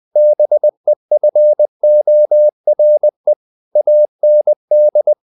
Morse code, heard as a keyed tone, about 20 wpm, 600 Hz. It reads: BEFORE AND
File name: the old man and the sea part 2 20wpm